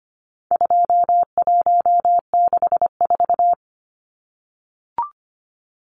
Morse code, heard as a keyed tone, about 25 wpm, 700 Hz. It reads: 2164
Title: Numbers - 4-Digit 25wpm